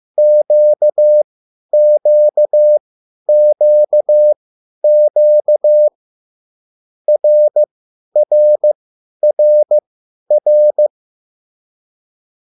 Morse code, heard as a keyed tone, about 15 wpm, spaced as for 10 wpm, 600 Hz. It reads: QQQQ RRRR
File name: cwbook